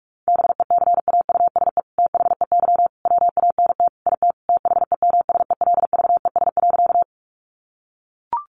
Morse code, heard as a keyed tone, about 44 wpm, 700 Hz. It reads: BECAUSE THEY WANT IT THEMSELVES.
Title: Sentences from Top 800 Words 45wpm